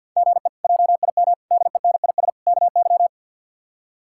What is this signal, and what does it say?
Morse code, heard as a keyed tone, about 51 wpm, 700 Hz. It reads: GE JIM DENIS KY
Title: K1USN SST QSO Practice - 50wpm